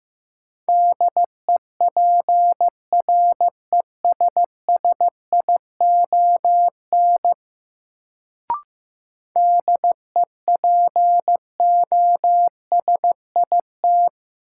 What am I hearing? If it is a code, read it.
Morse code, heard as a keyed tone, about 15 wpm, 700 Hz. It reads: DEPRESSION DEPOSIT
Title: Sentences from Top 1300 Words - New Word Drill - 15wpm